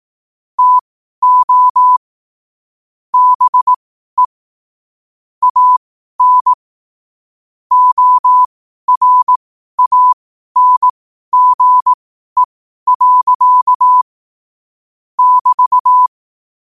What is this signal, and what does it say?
Morse code, heard as a keyed tone, about 18 wpm, spaced as for 12 wpm, 1000 Hz. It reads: TO BE AN ORANGE. <BT>
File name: qotdfiles / 12wpm